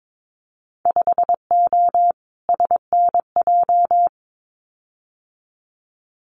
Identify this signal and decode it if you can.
Morse code, heard as a keyed tone, about 22 wpm, 700 Hz. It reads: 5O SNJ
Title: Winter Field Day Exchanges Encore 22wpm